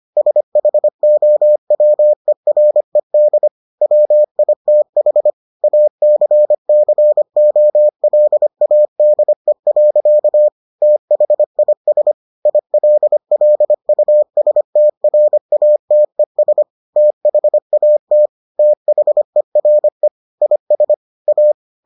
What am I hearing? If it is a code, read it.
Morse code, heard as a keyed tone, about 25 wpm, 600 Hz. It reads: SHOWERED WITH ACCOLADE. THIS ILLUSTRATES THAT THERE IS A